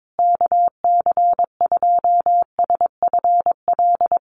Morse code, heard as a keyed tone, about 22 wpm, 700 Hz. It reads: KC2SFL
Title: USA Call Signs Repeated 2x 22wpm